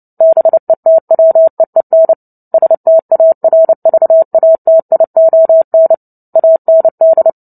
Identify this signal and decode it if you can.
Morse code, heard as a keyed tone, about 30 wpm, 650 Hz. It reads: BETWEEN STARVATION AND
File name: Book Two / 03-Chapter 3 - 29 WPM